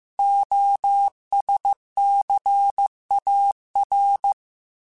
Morse code, heard as a keyed tone, about 15 wpm, 800 Hz. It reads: OSCAR